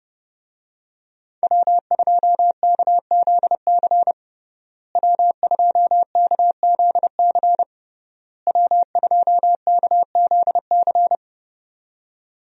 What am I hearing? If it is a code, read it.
Morse code, heard as a keyed tone, about 31 wpm, 700 Hz. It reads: W2KZC W2KZC W2KZC
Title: USA Call Signs - 5 Characters - Repeated 3x - 30wpm